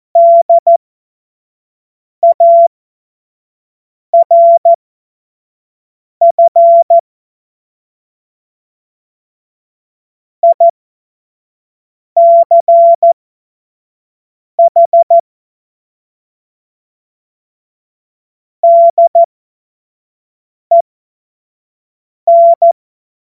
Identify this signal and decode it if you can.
Morse code, heard as a keyed tone, about 14 wpm, spaced as for 5 wpm, 680 Hz. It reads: DARF ICH DEN
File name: Morsekurs / Witze / Witz 11. Juli 2023 25 F70